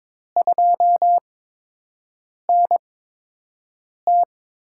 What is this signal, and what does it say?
Morse code, heard as a keyed tone, about 22 wpm, spaced as for 6 wpm, 700 Hz. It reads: 2NT
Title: USA Call Signs - 4 Characters - 8x Character Spacing - 22wpm